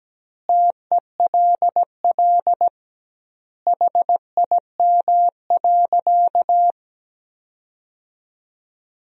Morse code, heard as a keyed tone, about 17 wpm, 700 Hz. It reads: TELL HIM.
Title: 2x Word-Spacing - Sentences from Top 200 Words - 17wpm